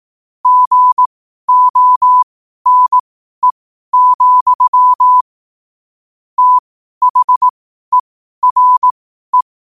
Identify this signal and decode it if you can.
Morse code, heard as a keyed tone, about 18 wpm, spaced as for 12 wpm, 1000 Hz. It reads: GONE, THERE
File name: qotdfiles / 12wpm